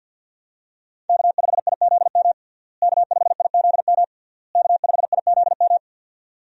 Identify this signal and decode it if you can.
Morse code, heard as a keyed tone, about 52 wpm, 700 Hz. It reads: K5IZM K5IZM K5IZM
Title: USA Call Signs - 5 Characters - Repeated 3x - 50wpm